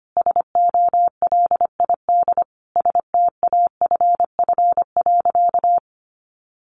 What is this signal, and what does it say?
Morse code, heard as a keyed tone, about 25 wpm, 700 Hz. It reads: SOLID STAFF.
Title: Sentences from Top 1500 Words 25wpm